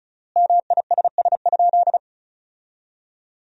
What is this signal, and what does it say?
Morse code, heard as a keyed tone, about 34 wpm, 700 Hz. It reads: MISS?
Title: 3x Word-Spacing - Sentences from Top 500 Words - 35wpm